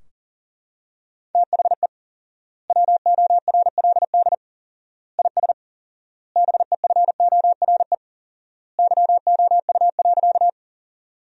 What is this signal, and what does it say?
Morse code, heard as a keyed tone, about 40 wpm, 700 Hz. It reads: THE WORLD IS BEFORE YOU.
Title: 4x Word-Spacing - Sentences from Top 200 Words - 40wpm